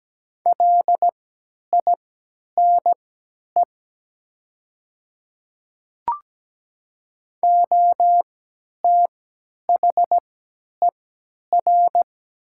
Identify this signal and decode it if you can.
Morse code, heard as a keyed tone, about 17 wpm, spaced as for 10 wpm, 700 Hz. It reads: LINE OTHER
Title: 3x Character Spacing - Top 300 Words - 17wpm